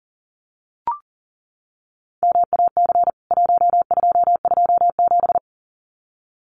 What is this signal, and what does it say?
Morse code, heard as a keyed tone, about 40 wpm, 700 Hz. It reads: MAC 1227
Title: CWT Exchange 40wpm